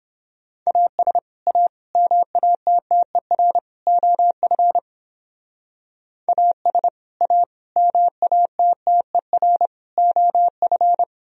Morse code, heard as a keyed tone, about 31 wpm, 700 Hz. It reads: AS A MATTER OF AS A MATTER OF